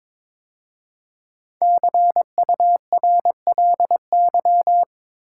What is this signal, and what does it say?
Morse code, heard as a keyed tone, about 22 wpm, 700 Hz. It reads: CURLY
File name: 5 Letter Words - First 3 Letters Given - 22wpm